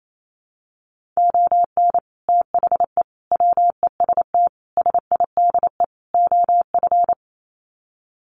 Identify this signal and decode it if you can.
Morse code, heard as a keyed tone, about 28 wpm, 700 Hz. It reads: ON THE WEST SIDE OF
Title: Sets of 5 Words Encore2 28wpm